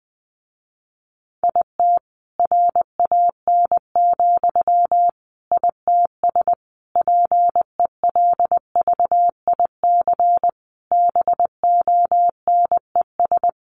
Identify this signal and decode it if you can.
Morse code, heard as a keyed tone, about 20 wpm, 700 Hz. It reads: IT RAN, ITS PELVIC BONES
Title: A Sound Of Thunder 20wpm